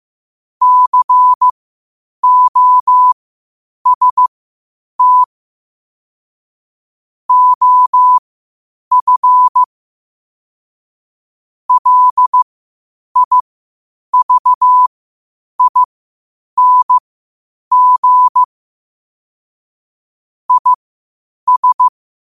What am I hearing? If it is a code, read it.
Morse code, heard as a keyed tone, about 15 wpm, spaced as for 8 wpm, 1000 Hz. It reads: COST OF LIVING IS